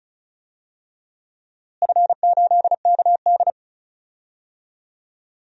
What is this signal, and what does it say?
Morse code, heard as a keyed tone, about 34 wpm, 700 Hz. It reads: F8KD
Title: DXCC-100 Call Signs - 4 Characters - Repeated 2x - 35wpm